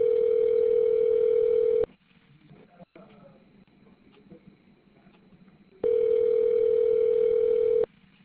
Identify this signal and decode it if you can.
Ringback tone (North America), 440 + 480 Hz.